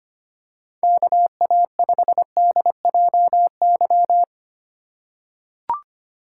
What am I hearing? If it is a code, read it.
Morse code, heard as a keyed tone, about 25 wpm, 700 Hz. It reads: KA5DJY
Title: USA Call Signs - 6 Characters - 8x Character Spacing - 25wpm